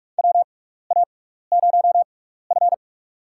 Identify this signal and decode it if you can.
Morse code, heard as a keyed tone, about 46 wpm, spaced as for 16 wpm, 700 Hz. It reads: WA0F
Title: USA Call Signs - 4 Characters - 6x Character Spacing - 45wpm